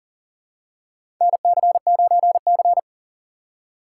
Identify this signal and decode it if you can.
Morse code, heard as a keyed tone, about 40 wpm, 700 Hz. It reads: NC9C
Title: USA Call Signs - 4 Characters - 3x Character Spacing - 40wpm